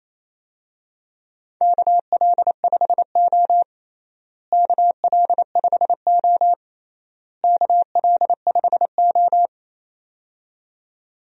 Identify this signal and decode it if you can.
Morse code, heard as a keyed tone, about 28 wpm, 700 Hz. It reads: KL5O KL5O KL5O
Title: USA Call Signs - 4 Characters - Repeated 3x - 28wpm